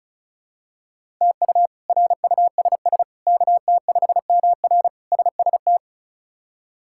Morse code, heard as a keyed tone, about 36 wpm, 700 Hz. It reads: TU RUSS KT5MR SST
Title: K1USN SST QSO Practice - 35wpm